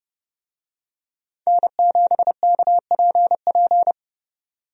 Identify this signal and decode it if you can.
Morse code, heard as a keyed tone, about 30 wpm, 700 Hz. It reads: N7KPP
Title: USA Call Signs - 5 Characters - 30wpm